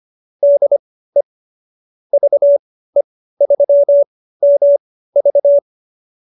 Morse code, heard as a keyed tone, about 25 wpm, spaced as for 15 wpm, 570 Hz. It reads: DE VE3MV